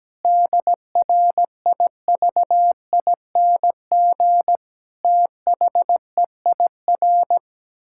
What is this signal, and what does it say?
Morse code, heard as a keyed tone, about 17 wpm, 700 Hz. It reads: DRIVING THEIR
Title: Pangram Challenge v2 - 17wpm